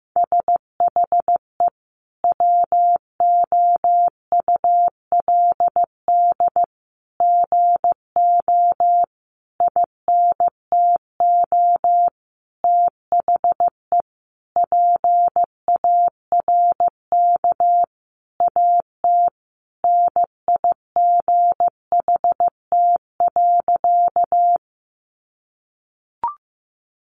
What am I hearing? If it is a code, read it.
Morse code, heard as a keyed tone, about 15 wpm, 700 Hz. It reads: SHE WOULD GO INTO THE PARK AT NIGHT.
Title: Sentences from Top 1300 Words 15wpm